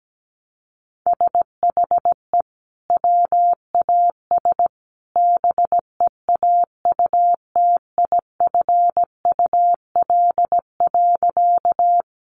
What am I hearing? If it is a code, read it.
Morse code, heard as a keyed tone, about 17 wpm, 700 Hz. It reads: SHE WAS BEAUTIFUL.
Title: Sentences from Top 900 Words 17wpm